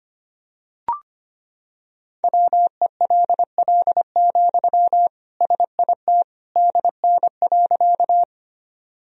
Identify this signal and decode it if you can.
Morse code, heard as a keyed tone, about 25 wpm, 700 Hz. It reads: WELL, SIT DN.
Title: Sentences from Top 500 Words and Abbreviations 25wpm